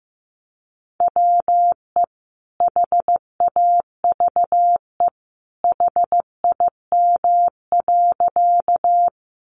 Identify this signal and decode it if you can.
Morse code, heard as a keyed tone, about 15 wpm, 700 Hz. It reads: WE HAVE HIM.
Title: Sentences from Top 100 Words 15wpm